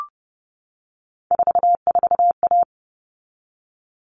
Morse code, heard as a keyed tone, about 30 wpm, 700 Hz. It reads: 44A
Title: All Cut Numbers - 3-Digit 30wpm